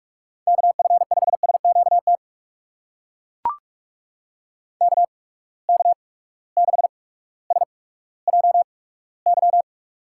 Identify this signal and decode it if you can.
Morse code, heard as a keyed tone, about 42 wpm, spaced as for 19 wpm, 700 Hz. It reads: KF5SQT KK6SJY
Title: USA Call Signs - 6 Characters - 8x Character Spacing - 45wpm